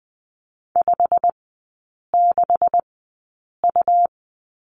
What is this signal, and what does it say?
Morse code, heard as a keyed tone, about 19 wpm, spaced as for 8 wpm, 690 Hz. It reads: 56U
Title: 166 Lesson 27e - Number 7 - Three Characters 20wpm